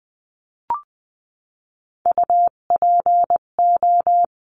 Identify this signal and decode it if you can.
Morse code, heard as a keyed tone, about 20 wpm, 700 Hz. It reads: UPO
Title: 258 Lesson 42c - Words 16-20wpm